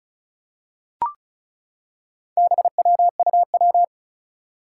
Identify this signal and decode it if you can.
Morse code, heard as a keyed tone, about 36 wpm, 700 Hz. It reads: BWUW
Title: Four Challenging Letter-Number Combinations 35wpm